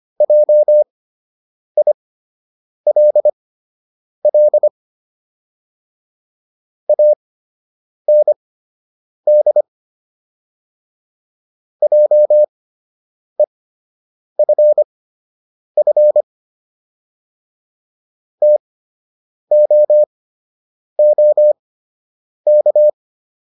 Morse code, heard as a keyed tone, about 25 wpm, spaced as for 8 wpm, 600 Hz. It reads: JILL AND JEFF TOOK